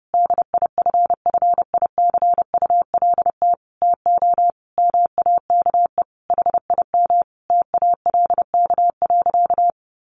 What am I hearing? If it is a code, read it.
Morse code, heard as a keyed tone, about 28 wpm, 700 Hz. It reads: DIFFICULT TO MAKE HIM TALK.
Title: Sentences from Top 600 Words 30wpm